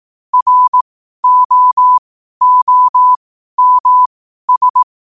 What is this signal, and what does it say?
Morse code, heard as a keyed tone, about 18 wpm, spaced as for 12 wpm, 1000 Hz. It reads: ROOMS
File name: qotdfiles / 12wpm